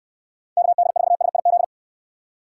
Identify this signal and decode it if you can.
Morse code, heard as a keyed tone, about 68 wpm, 700 Hz. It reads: KD4SIZ